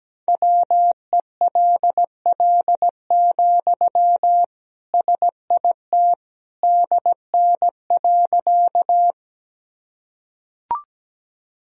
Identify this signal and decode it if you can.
Morse code, heard as a keyed tone, about 17 wpm, 700 Hz. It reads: WELL, SIT DN.
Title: Sentences from Top 500 Words and Abbreviations 17wpm